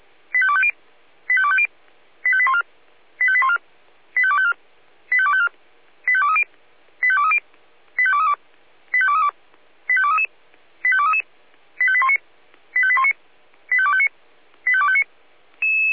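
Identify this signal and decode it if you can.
Five-tone selective call (ZVEI), heard as quick runs of five tones, tones of about 70 ms. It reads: 85250 85250 85814 85814 85254 85254 85230 85230 85232 85232 85233 85233 85819 85819 85259 85259
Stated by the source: ZVEI five-tone selective call received by SDR.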